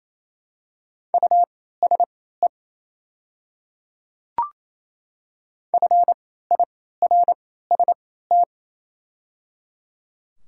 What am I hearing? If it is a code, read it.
Morse code, heard as a keyed tone, about 27 wpm, spaced as for 16 wpm, 690 Hz. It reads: USE FIRST
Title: 3x Character Spacing - Top 100 Words - 28wpm